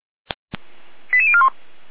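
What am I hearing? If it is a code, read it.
Five-tone selective call (ZVEI), heard as quick runs of five tones, tones of about 70 ms. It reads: 88041
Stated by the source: ZVEI five-tone selective call 88074, received via SDR.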